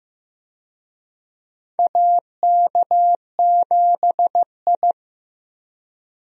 Morse code, heard as a keyed tone, about 15 wpm, 700 Hz. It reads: AK7I
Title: USA Call Signs - 4 Characters - Repeated 2x - 15wpm